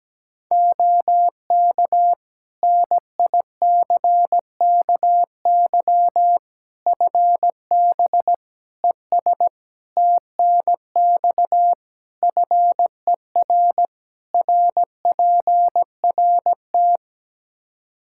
Morse code, heard as a keyed tone, about 17 wpm, 700 Hz. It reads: OK NICKY FB ES TNX FER RPRT
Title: Practice QSO - Exchange 2 - Info - 17wpm